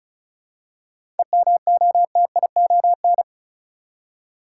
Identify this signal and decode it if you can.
Morse code, heard as a keyed tone, about 34 wpm, 700 Hz. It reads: EMOTION
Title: Seven Letter Words - 35wpm